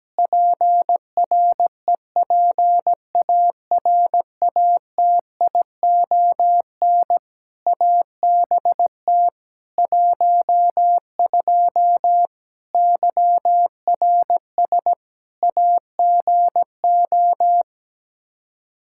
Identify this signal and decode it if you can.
Morse code, heard as a keyed tone, about 17 wpm, 700 Hz. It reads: PREPARATION ABT 12 YRS AGO
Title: K1USN SST QSO Practice 17wpm